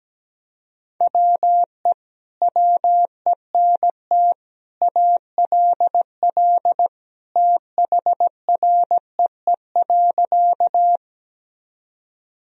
Morse code, heard as a keyed tone, about 17 wpm, 700 Hz. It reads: WE WENT ALL THREE.
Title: Sentences from Top 1000 Words - Review Part 1 - 17wpm